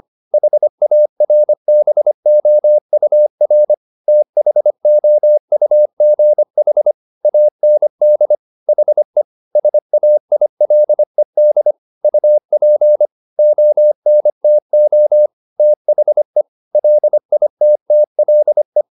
Morse code, heard as a keyed tone, about 25 wpm, 600 Hz. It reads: HARBOUR THOUGH AND HE SAILED UP ONTO THE LITTLE